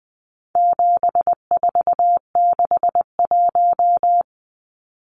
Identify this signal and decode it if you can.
Morse code, heard as a keyed tone, about 20 wpm, 700 Hz. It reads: 7461
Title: Numbers - 4-Digit 20wpm